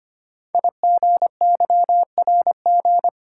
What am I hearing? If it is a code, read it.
Morse code, heard as a keyed tone, about 25 wpm, 700 Hz. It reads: IGYRG
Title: Code Groups - 5 Letters - 25wpm